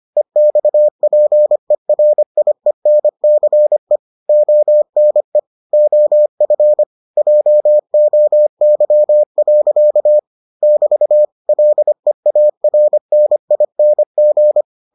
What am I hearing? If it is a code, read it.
Morse code, heard as a keyed tone, about 25 wpm, 600 Hz. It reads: EXPERIENCE ONE OF JOY. <BT> LEARNING